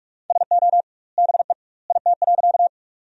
Morse code, heard as a keyed tone, about 43 wpm, 700 Hz. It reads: SO BE IT.
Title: 2x Word-Spacing - Sentences from Top 100 Words - 45wpm